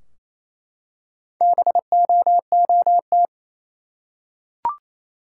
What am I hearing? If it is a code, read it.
Morse code, heard as a keyed tone, about 28 wpm, 700 Hz. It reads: BOOT
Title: Four Letter Words - 28wpm